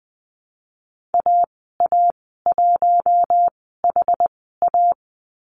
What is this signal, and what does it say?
Morse code, heard as a keyed tone, about 20 wpm, spaced as for 15 wpm, 700 Hz. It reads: AA1HA
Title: 062 Lesson 11g - Letter h - Call Signs 20wpm